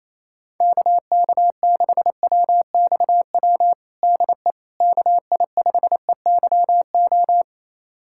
Morse code, heard as a keyed tone, about 28 wpm, 700 Hz. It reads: KK6WXW DE KI5EYO
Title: USA Callsign DE Callsign - 6 Characters - 28wpm